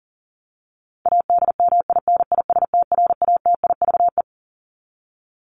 Morse code, heard as a keyed tone, about 40 wpm, 700 Hz. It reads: ADMINISTRATIVE
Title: Top 1501-1750 Words 40wpm